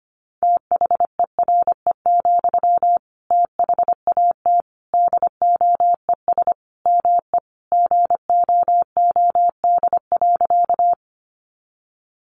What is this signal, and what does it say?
Morse code, heard as a keyed tone, about 25 wpm, 700 Hz. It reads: THERE, THAT DOES ME GOOD.
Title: Sentences from Top 200 Words 25wpm